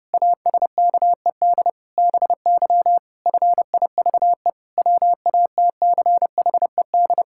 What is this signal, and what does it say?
Morse code, heard as a keyed tone, about 30 wpm, 700 Hz. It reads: ASKED BY FIVE WATCHED
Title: Pangram Challenge v2 - 30wpm